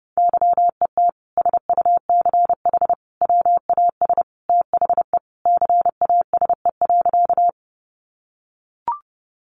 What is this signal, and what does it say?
Morse code, heard as a keyed tone, about 30 wpm, 700 Hz. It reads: YET SUCH WAS THE CASE.
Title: Sentences from Top 700 Words 30wpm